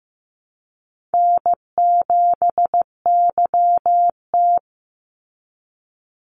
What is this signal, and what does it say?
Morse code, heard as a keyed tone, about 15 wpm, 700 Hz. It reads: N7YT
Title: Call Signs Repeated 3x - 15wpm